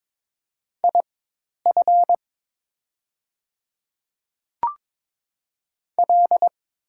Morse code, heard as a keyed tone, about 22 wpm, spaced as for 10 wpm, 690 Hz. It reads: IF L
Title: 4x Character Spacing - Top 100 Words - 22wpm